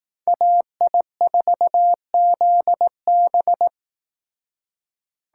Morse code, heard as a keyed tone, about 19 wpm, 700 Hz. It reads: AI4ZB